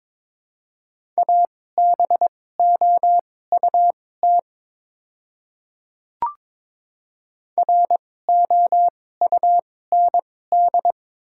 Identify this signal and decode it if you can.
Morse code, heard as a keyed tone, about 22 wpm, spaced as for 16 wpm, 700 Hz. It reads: ABOUT ROUND
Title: 2x Character Spacing - Top 200 Words - 22wpm